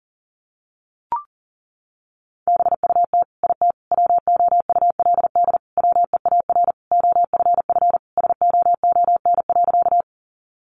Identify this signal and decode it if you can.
Morse code, heard as a keyed tone, about 40 wpm, 700 Hz. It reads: BUT IT WOULD WEAR OFF SOON.
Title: Sentences from Top 1500 Words 40wpm